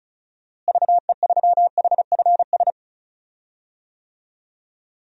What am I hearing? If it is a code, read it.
Morse code, heard as a keyed tone, about 36 wpm, 700 Hz. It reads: VE3HFS
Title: DXCC-100 Call Signs - 6 Characters - Repeated 3x - 35wpm